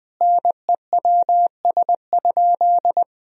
Morse code, heard as a keyed tone, about 20 wpm, 700 Hz. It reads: NEWS?
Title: Sentences from Top 1000 Words - Review Part 4 - 20wpm